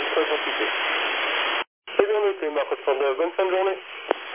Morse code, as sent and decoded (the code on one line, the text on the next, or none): none